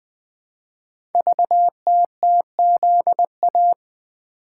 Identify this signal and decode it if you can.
Morse code, heard as a keyed tone, about 21 wpm, 700 Hz. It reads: VTTZA
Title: Code Groups - 5 Letters - 20wpm